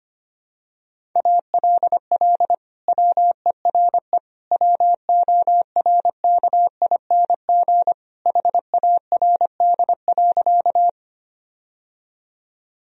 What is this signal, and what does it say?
Morse code, heard as a keyed tone, about 25 wpm, 700 Hz. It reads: ALL WERE WORKING HARD.
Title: Sentences from Top 1000 Words - Review Part 4 - 25wpm